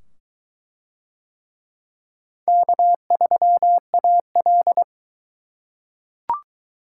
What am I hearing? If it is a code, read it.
Morse code, heard as a keyed tone, about 21 wpm, 700 Hz. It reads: K3AL